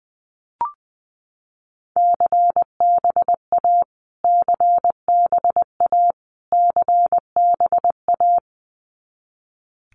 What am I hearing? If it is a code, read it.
Morse code, heard as a keyed tone, about 20 wpm, 700 Hz. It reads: CBA CBA CBA
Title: Abbreviations Repeated 3x - 20wpm